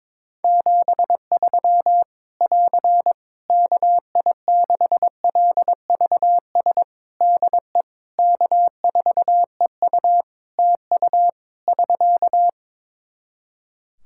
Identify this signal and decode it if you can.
Morse code, heard as a keyed tone, about 21 wpm, 700 Hz. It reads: 73 <AR> KI6LVS DE K4EU TU <SK>
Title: K1USN SST QSO Practice 22wpm